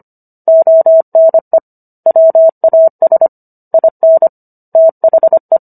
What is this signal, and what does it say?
Morse code, heard as a keyed tone, about 25 wpm, 650 Hz. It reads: ONE WAS IN THE